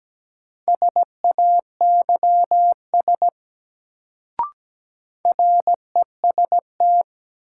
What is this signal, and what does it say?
Morse code, heard as a keyed tone, about 17 wpm, 700 Hz. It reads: SAYS REST
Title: Sets of 2 Words - New Word Drill - 17wpm